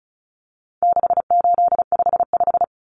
Morse code, heard as a keyed tone, about 34 wpm, 700 Hz. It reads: 6855
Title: Common Cut Numbers - 4-Digit 35wpm